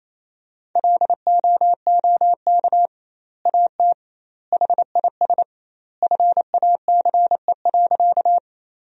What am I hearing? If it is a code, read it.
Morse code, heard as a keyed tone, about 28 wpm, 700 Hz. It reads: LOOK AT HIS FACE.
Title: 2x Word-Spacing - Sentences from Top 300 Words - 28wpm